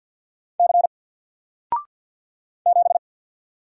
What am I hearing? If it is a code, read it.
Morse code, heard as a keyed tone, about 48 wpm, 700 Hz. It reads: C 7
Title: Single Character (Letter:Number:Punctuation:Prosign) 50wpm